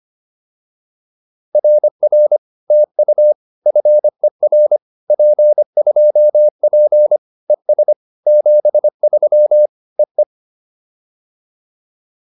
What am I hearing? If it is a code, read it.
Morse code, heard as a keyed tone, about 25 wpm, 600 Hz. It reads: RR TU FER P2P ES 73 EE